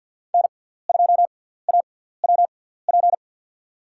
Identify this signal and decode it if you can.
Morse code, heard as a keyed tone, about 49 wpm, spaced as for 17 wpm, 690 Hz. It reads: N2AWP
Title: USA Call Signs - 6x Character Spacing - 50wpm